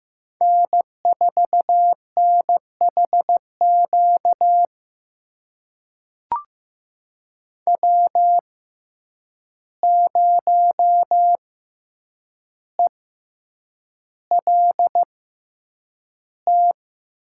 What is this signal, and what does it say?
Morse code, heard as a keyed tone, about 15 wpm, spaced as for 8 wpm, 700 Hz. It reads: N4NHQ W0ELT
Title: USA Call Signs - 5 Characters - 6x Character Spacing - 15wpm